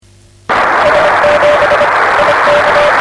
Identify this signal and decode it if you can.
Morse code, heard as a keyed tone, about 25 wpm, 560 Hz, with noise. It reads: S7 IK
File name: RS-7 Beacon 16 Feb 1985 oz6bl